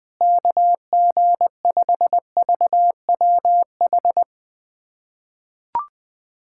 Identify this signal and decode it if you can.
Morse code, heard as a keyed tone, about 20 wpm, 700 Hz. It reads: KG5VWH
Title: USA Callsign DE Callsign 20wpm